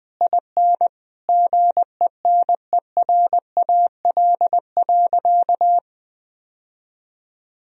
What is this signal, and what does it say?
Morse code, heard as a keyed tone, about 20 wpm, 700 Hz. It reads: IN GENERAL.
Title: Sentences from Top 1000 Words - Review Part 3 - 20wpm